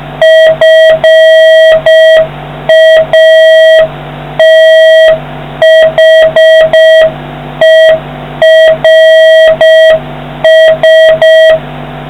Morse code, heard as a keyed tone, about 6 wpm, 1950 Hz, with noise. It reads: FATHERS